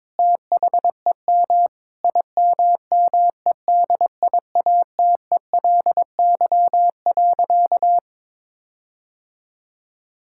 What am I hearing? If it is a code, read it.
Morse code, heard as a keyed tone, about 22 wpm, 700 Hz. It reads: THEM IMMEDIATELY.
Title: Sentences from Top 1600 Words 22wpm